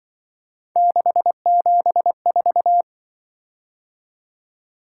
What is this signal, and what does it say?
Morse code, heard as a keyed tone, about 23 wpm, 700 Hz. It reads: 674
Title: Speed-Racing - Three Letter-Number Combinations - 20wpm